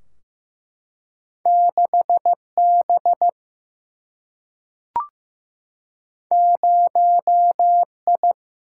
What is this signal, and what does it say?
Morse code, heard as a keyed tone, about 15 wpm, 700 Hz. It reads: 6 B 0 I
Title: Single-Letter Single-Number 15wpm